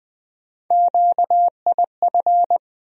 Morse code, heard as a keyed tone, about 21 wpm, 700 Hz. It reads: QIF
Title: USA Call Sign Postfixes - 3 Letters 20wpm